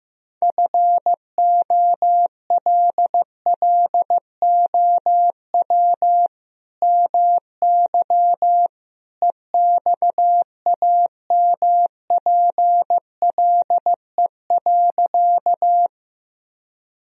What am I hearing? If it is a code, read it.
Morse code, heard as a keyed tone, about 15 wpm, 700 Hz. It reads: FOLLOW MY EXAMPLE.
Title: Sentences from Top 1000 Words - Review Part 1 - 15wpm